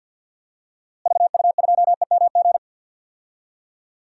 Morse code, heard as a keyed tone, about 50 wpm, 700 Hz. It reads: VU2EMG
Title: DXCC-100 Call Signs - 6 Characters - Repeated 2x - 50wpm